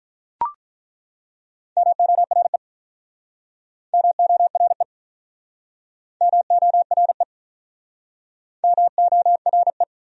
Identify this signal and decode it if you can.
Morse code, heard as a keyed tone, about 48 wpm, 700 Hz. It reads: MORE MORE MORE MORE